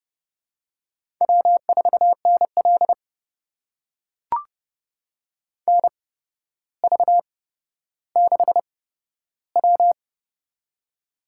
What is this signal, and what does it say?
Morse code, heard as a keyed tone, about 32 wpm, spaced as for 13 wpm, 700 Hz. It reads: W4NL NV6W
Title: USA Call Signs - 8x Character Spacing - 30wpm